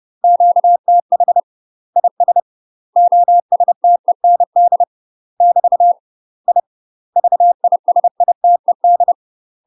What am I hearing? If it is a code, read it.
Morse code, heard as a keyed tone, about 30 wpm, 700 Hz. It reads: QTH IS OSTEND <BT> I VISITED
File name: UK MASTER QSOs TRAINING / 30WPM / UK MASTER QSOs 0003